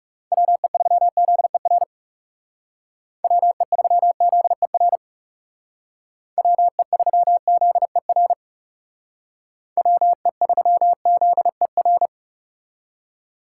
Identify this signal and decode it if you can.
Morse code, heard as a keyed tone, about 39 wpm, 700 Hz. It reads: WE3ZER WE3ZER WE3ZER WE3ZER